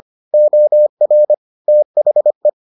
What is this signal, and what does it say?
Morse code, heard as a keyed tone, about 25 wpm, 600 Hz. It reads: OR THE